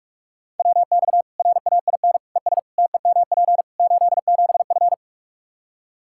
Morse code, heard as a keyed tone, about 46 wpm, 700 Hz. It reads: WX RAIN ES TEMP 87F
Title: Practice QSO - Exchange 2 - Info - 45wpm